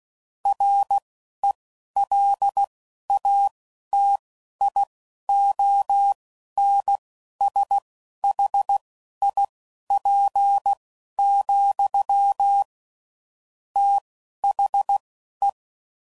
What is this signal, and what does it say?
Morse code, heard as a keyed tone, about 16 wpm, spaced as for 12 wpm, 800 Hz. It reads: RELATIONSHIP, THE
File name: CW Code Groups @10WPM / Technical Subjects / XC